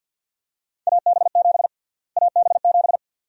Morse code, heard as a keyed tone, about 51 wpm, 700 Hz. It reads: AB7 AB7
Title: USA Call Sign Prefixes - Repeated 2x - 50wpm